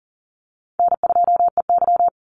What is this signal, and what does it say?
Morse code, heard as a keyed tone, about 40 wpm, 700 Hz. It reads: N2EY
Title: Easy Call Signs - 40wpm